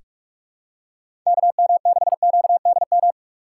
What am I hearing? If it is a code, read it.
Morse code, heard as a keyed tone, about 46 wpm, 700 Hz. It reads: KM6QDM
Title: USA Call Signs - 6 Characters - 6x Character Spacing - 45wpm